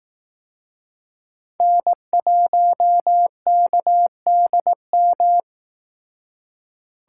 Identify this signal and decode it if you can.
Morse code, heard as a keyed tone, about 18 wpm, 700 Hz. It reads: N1KDM
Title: Speed Racing - USA 1x3 Call Signs 15wpm